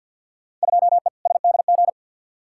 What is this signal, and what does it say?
Morse code, heard as a keyed tone, about 46 wpm, 700 Hz. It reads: 2E SDG
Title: Field Day Exchanges - 50wpm